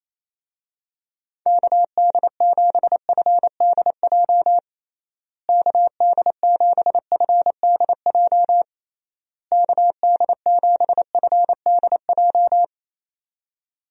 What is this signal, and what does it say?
Morse code, heard as a keyed tone, about 28 wpm, 700 Hz. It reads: KD7FDJ KD7FDJ KD7FDJ
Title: USA Call Signs - 6 Characters - Repeated 3x - 28wpm